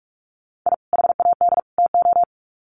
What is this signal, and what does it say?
Morse code, heard as a keyed tone, about 44 wpm, 700 Hz. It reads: I HAD TO
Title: Sets of 3 Words 45wpm